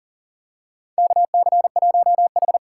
Morse code, heard as a keyed tone, about 40 wpm, 700 Hz. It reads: KC1H
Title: USA Call Signs - 4 Characters - 6x Character Spacing - 40wpm